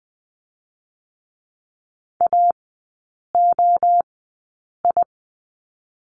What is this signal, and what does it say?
Morse code, heard as a keyed tone, about 20 wpm, 700 Hz. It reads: A O I
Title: 029 Lesson 6e - Letter i - Three Characters 20wpm